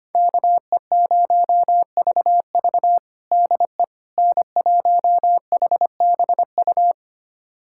Morse code, heard as a keyed tone, about 25 wpm, 700 Hz. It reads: KE0VV DE N1HBU
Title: USA Callsign DE Callsign - 5 Characters - 25wpm